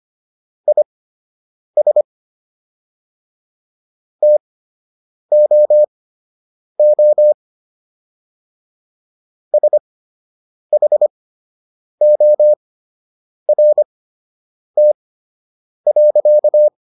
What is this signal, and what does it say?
Morse code, heard as a keyed tone, about 25 wpm, spaced as for 8 wpm, 600 Hz. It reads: IS TOO SHORT.